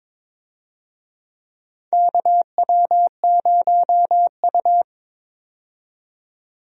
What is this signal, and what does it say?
Morse code, heard as a keyed tone, about 22 wpm, 700 Hz. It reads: KW0U
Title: USA Call Signs - 4 Characters - 22wpm